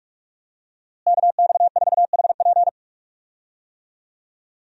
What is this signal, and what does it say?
Morse code, heard as a keyed tone, about 46 wpm, 700 Hz. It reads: KX4HP
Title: USA Call Signs - 5 Characters - Repeated 3x - 45wpm